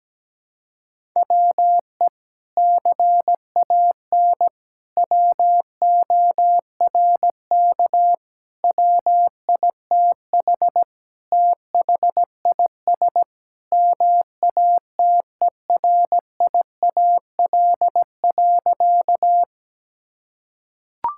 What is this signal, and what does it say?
Morse code, heard as a keyed tone, about 17 wpm, 700 Hz. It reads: WE CAN WORK WITH THIS MATERIAL.
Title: Sentences from Top 1000 Words - Review Part 5 - 17wpm